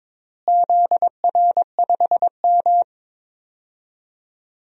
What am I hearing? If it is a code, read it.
Morse code, heard as a keyed tone, about 22 wpm, 700 Hz. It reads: ZR5M
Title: DXCC-100 Call Signs - 4 Characters - Repeated 3x - 22wpm